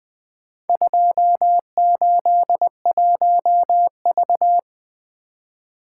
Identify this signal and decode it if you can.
Morse code, heard as a keyed tone, about 20 wpm, 700 Hz. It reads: 281V
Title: Four Letter-Number Combinations - 20wpm